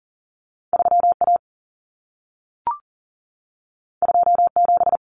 Morse code, heard as a keyed tone, about 40 wpm, 700 Hz. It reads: 3A 27
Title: Common Cut Numbers - 2-Digit 40wpm